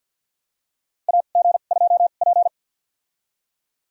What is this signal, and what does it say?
Morse code, heard as a keyed tone, about 51 wpm, spaced as for 36 wpm, 700 Hz. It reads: AG2P
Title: USA Call Signs - 2x Character Spacing - 50wpm